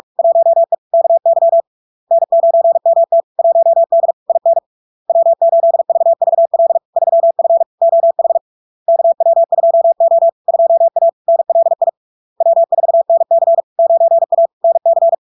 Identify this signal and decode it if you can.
Morse code, heard as a keyed tone, about 46 wpm, 670 Hz. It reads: 1E KY N9MT 1D IN W8VVL 3F OH KW2O 2A NLI W4NC 9A NC
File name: LARK-FD-2022-Log-in-CW-45wpm-670-Hz